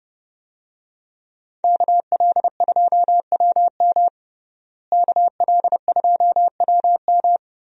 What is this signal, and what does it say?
Morse code, heard as a keyed tone, about 30 wpm, 700 Hz. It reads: KL2WM KL2WM
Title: USA Call Signs - 5 Characters - Repeated 3x - 30wpm